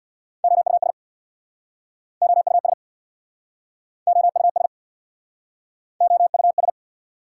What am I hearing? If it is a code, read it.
Morse code, heard as a keyed tone, about 69 wpm, 690 Hz. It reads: OUS OUS OUS OUS